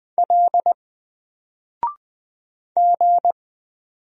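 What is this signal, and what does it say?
Morse code, heard as a keyed tone, about 20 wpm, 700 Hz. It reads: L G
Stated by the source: Single letter and number Morse drill at 20 wpm, from N4LCD.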